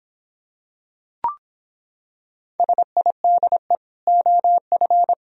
Morse code, heard as a keyed tone, about 26 wpm, 700 Hz. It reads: SIDE OF